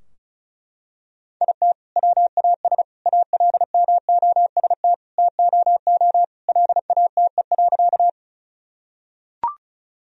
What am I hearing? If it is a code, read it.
Morse code, heard as a keyed tone, about 36 wpm, 700 Hz. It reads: IT WAS ALMOST TOO LATE.
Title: Sentences from Top 1000 Words - Review Part 4 - 35wpm